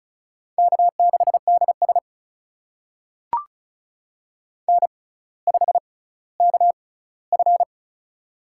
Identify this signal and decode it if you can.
Morse code, heard as a keyed tone, about 36 wpm, spaced as for 18 wpm, 700 Hz. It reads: K6DS N5KF
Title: USA Call Signs - 4 Characters - 6x Character Spacing - 35wpm